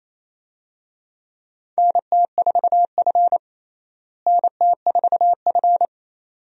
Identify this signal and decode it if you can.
Morse code, heard as a keyed tone, about 27 wpm, 700 Hz. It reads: NT4F NT4F
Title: USA Call Signs - 4 Characters - Repeated 2x - 28wpm